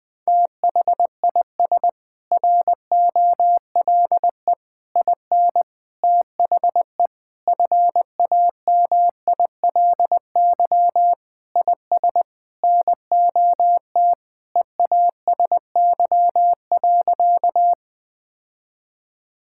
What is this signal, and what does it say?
Morse code, heard as a keyed tone, about 20 wpm, 700 Hz. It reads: THIS ROLE IN THE FAMILY IS NOT EASY.
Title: Sentences from Top 1000 Words - Review Part 6 - 20wpm